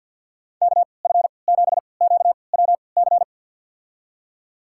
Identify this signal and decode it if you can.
Morse code, heard as a keyed tone, about 53 wpm, spaced as for 29 wpm, 700 Hz. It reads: KF7QWC
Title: USA Call Signs - 3x Character Spacing - 50wpm